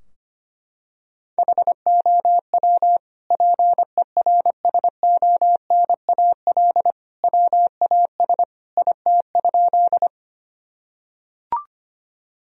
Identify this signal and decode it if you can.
Morse code, heard as a keyed tone, about 25 wpm, 700 Hz. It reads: HOW PERSONAL WAS IT?
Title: Sentences from Top 1000 Words - Review Part 4 - 25wpm